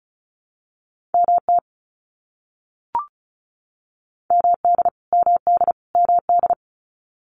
Morse code, heard as a keyed tone, about 34 wpm, 700 Hz. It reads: MT MD MD MD
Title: US States Repeated 3x 35wpm